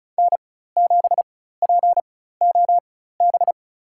Morse code, heard as a keyed tone, about 34 wpm, spaced as for 16 wpm, 700 Hz. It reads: N7POB
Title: USA Call Signs - 5 Characters - 4x Character Spacing - 35wpm